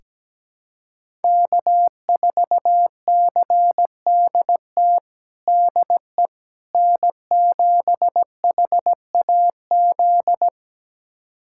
Morse code, heard as a keyed tone, about 17 wpm, 700 Hz. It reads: K4CDT DE N7HAZ
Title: USA Callsign DE Callsign - 5 Characters - 17wpm